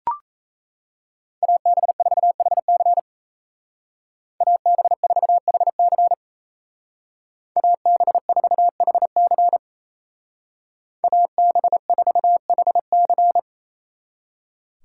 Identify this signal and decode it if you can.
Morse code, heard as a keyed tone, about 34 wpm, 700 Hz. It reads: AB4HC AB4HC AB4HC AB4HC
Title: Speed Racing - USA 2x2 Call Signs 28wpm